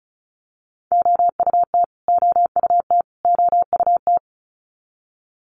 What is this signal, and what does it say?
Morse code, heard as a keyed tone, about 34 wpm, 700 Hz. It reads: OUT OUT OUT
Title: Top 200 Words Repeated 35wpm